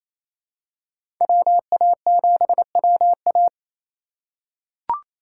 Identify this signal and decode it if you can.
Morse code, heard as a keyed tone, about 28 wpm, 700 Hz. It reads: WA7WA
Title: USA Call Signs - 5 Characters - 2x Character Spacing - 28wpm